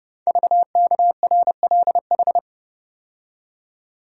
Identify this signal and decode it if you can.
Morse code, heard as a keyed tone, about 30 wpm, 700 Hz. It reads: VKRLH
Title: Code Groups - 5 Letters - 30wpm